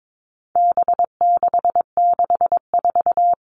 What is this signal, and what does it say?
Morse code, heard as a keyed tone, about 22 wpm, 700 Hz. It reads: B664
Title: All Cut Numbers - 4-Digit 22wpm